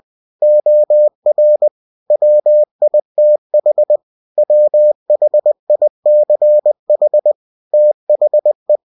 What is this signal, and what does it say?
Morse code, heard as a keyed tone, about 20 wpm, 600 Hz. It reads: OR WITH WHICH THE
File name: cw-books / winnie-the-pooh / winnie-20wpm-page-0115